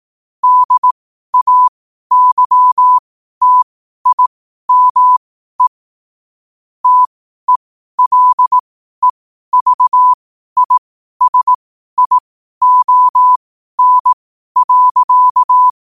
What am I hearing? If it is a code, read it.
Morse code, heard as a keyed tone, about 18 wpm, spaced as for 12 wpm, 1000 Hz. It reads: DAYTIME TELEVISION.
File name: qotdfiles / 12wpm